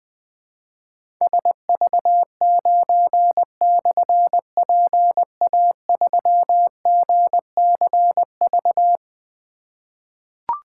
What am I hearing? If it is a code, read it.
Morse code, heard as a keyed tone, about 21 wpm, 700 Hz. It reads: SV9/PA3GCV
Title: Difficult Call Signs 20wpm